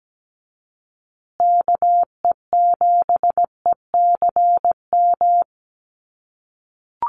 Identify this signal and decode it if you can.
Morse code, heard as a keyed tone, about 17 wpm, 700 Hz. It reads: KE7ECM
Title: USA Call Signs Repeated 2x 17wpm